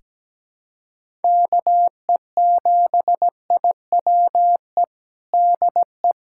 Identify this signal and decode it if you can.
Morse code, heard as a keyed tone, about 17 wpm, 700 Hz. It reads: KE7IWE DE
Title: USA Callsign DE Callsign 17wpm